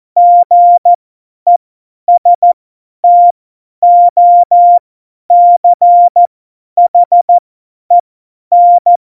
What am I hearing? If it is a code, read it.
Morse code, heard as a keyed tone, about 14 wpm, spaced as for 10 wpm, 700 Hz. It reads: GESTOCHEN
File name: Morsekurs / Witze / Witz 06 March 2018, 50 F70